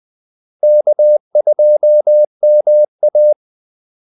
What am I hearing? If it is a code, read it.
Morse code, heard as a keyed tone, about 20 wpm, 600 Hz. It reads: K2MA
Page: Kaldesignaler 20 wpm